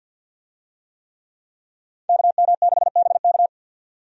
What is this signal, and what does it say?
Morse code, heard as a keyed tone, about 46 wpm, 700 Hz. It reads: KM6BK
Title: USA Call Signs - 5 Characters - 50wpm